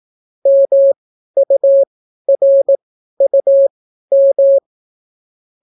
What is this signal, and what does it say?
Morse code, heard as a keyed tone, about 18 wpm, spaced as for 12 wpm, 550 Hz. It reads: MURUM